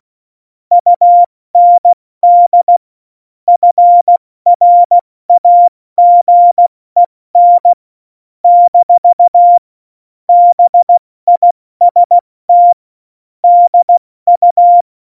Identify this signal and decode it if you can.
Morse code, heard as a keyed tone, about 16 wpm, 700 Hz. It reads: UND FRAGEN - BIST DU
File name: Morsekurs / Witze / Witz 19. Januar 2021, 70 F80